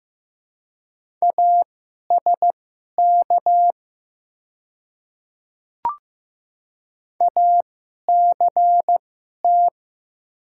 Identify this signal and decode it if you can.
Morse code, heard as a keyed tone, about 15 wpm, spaced as for 11 wpm, 700 Hz. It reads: ASK ACT
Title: 2x Character Spacing - Top 200 Words - 15wpm